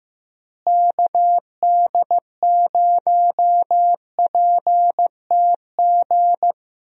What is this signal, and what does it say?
Morse code, heard as a keyed tone, about 15 wpm, 700 Hz. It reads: KD0PTG
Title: USA Call Signs - 6 Characters - Repeated 3x - 15wpm